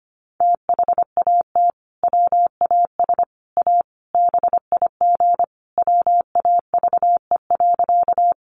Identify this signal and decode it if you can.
Morse code, heard as a keyed tone, about 25 wpm, 700 Hz. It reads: THAT WAS A BIG WAVE.
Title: Sentences from Top 1400 Words 25wpm